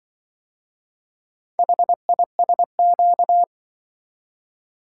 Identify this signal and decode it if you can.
Morse code, heard as a keyed tone, about 25 wpm, 700 Hz. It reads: HISQ
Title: Speed-Racing - Four Letter-Number Combinations - 20wpm